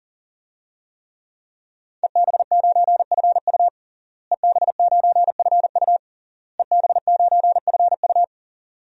Morse code, heard as a keyed tone, about 40 wpm, 700 Hz. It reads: EB9FU EB9FU EB9FU
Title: DXCC-100 Call Signs - 5 Characters - Repeated 3x - 40wpm